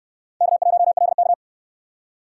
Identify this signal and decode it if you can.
Morse code, heard as a keyed tone, about 68 wpm, 700 Hz. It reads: K1LK